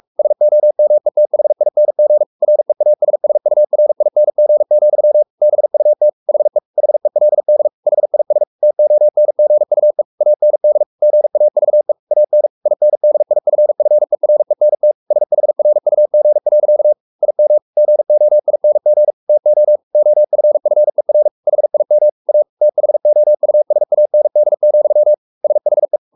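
Morse code, heard as a keyed tone, about 44 wpm, 600 Hz. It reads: SOMETHING REASSURING, BUT HE HELD HIS TONGUE AND GAVE AN INDIFFERENT SHRUG. IM GOING TO OFFER HIM A THOUSAND, SHE
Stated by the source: Morse code at 43 wpm sending Ken Follett's novel Never in English, from CW med Gnister.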